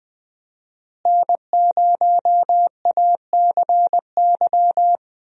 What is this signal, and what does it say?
Morse code, heard as a keyed tone, about 20 wpm, 700 Hz. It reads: N0ACY
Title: USA Call Signs - 5 Characters - Repeated 2x - 20wpm